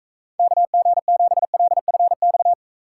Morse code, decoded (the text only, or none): KG7LFX